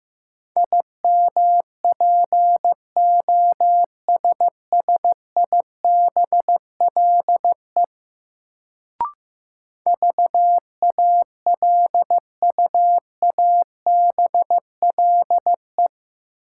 Morse code, heard as a keyed tone, about 15 wpm, 700 Hz. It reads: IMPOSSIBLE VALUABLE
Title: Sentences from Top 1100 Words - New Word Drill - 15wpm